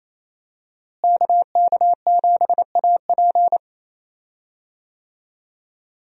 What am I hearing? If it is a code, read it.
Morse code, heard as a keyed tone, about 28 wpm, 700 Hz. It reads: KK7AP
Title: USA Call Signs - 5 Characters - Repeated 3x - 28wpm